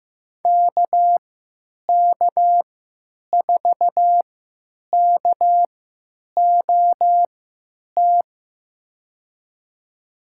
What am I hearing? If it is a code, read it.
Morse code, heard as a keyed tone, about 15 wpm, spaced as for 9 wpm, 700 Hz. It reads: KK4KOT
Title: USA Call Signs - 6 Characters - 3x Character Spacing - 15wpm